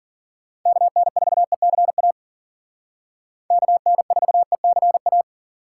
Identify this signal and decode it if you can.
Morse code, heard as a keyed tone, about 52 wpm, 700 Hz. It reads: KN4ECA KN4ECA